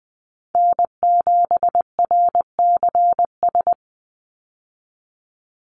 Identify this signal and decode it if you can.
Morse code, heard as a keyed tone, about 19 wpm, 700 Hz. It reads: N7RCS
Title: Call Signs Repeated 3x - 20wpm